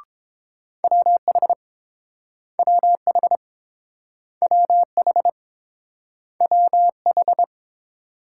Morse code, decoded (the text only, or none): WH WH WH WH